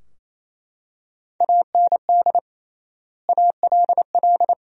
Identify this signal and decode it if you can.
Morse code, heard as a keyed tone, about 28 wpm, 700 Hz. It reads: AND ALL
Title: 3x Word-Spacing - Sentences from Top 100 Words - 28wpm